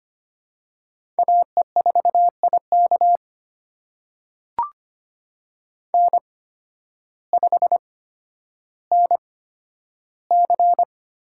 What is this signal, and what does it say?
Morse code, heard as a keyed tone, about 25 wpm, spaced as for 12 wpm, 700 Hz. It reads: AE4IK N5NC
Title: USA Call Signs - 5 Characters - 8x Character Spacing - 25wpm